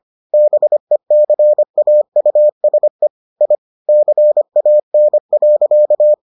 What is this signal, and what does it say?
Morse code, heard as a keyed tone, about 25 wpm, 600 Hz. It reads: BECAUSE I CAN.